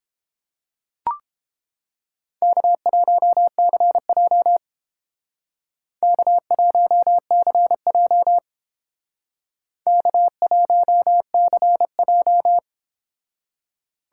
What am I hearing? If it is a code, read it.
Morse code, heard as a keyed tone, about 29 wpm, 700 Hz. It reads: K1CJ K1CJ K1CJ